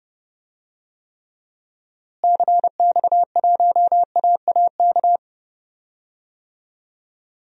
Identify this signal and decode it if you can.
Morse code, heard as a keyed tone, about 29 wpm, 700 Hz. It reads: CX1AAK
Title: DXCC-100 Call Signs - 6 Characters - Repeated 2x - 30wpm